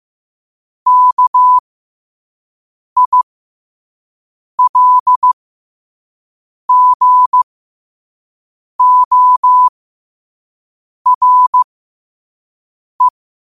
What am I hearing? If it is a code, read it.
Morse code, heard as a keyed tone, about 15 wpm, spaced as for 5 wpm, 1000 Hz. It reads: KILGORE
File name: qotdfiles / 5wpm